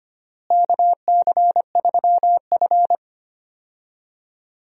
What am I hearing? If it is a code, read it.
Morse code, heard as a keyed tone, about 25 wpm, 700 Hz. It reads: KC3F
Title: USA Call Signs - 4 Characters - Repeated 3x - 25wpm